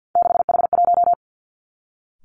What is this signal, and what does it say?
Morse code, heard as a keyed tone, about 50 wpm, 700 Hz. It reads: 6H1
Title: Three Challenging Letter-Number Combinations 50wpm